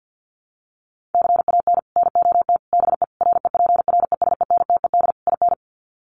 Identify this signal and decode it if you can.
Morse code, heard as a keyed tone, about 50 wpm, 700 Hz. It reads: CAN NOT BE REPRESENTED IN
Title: Sets of 5 Words 50wpm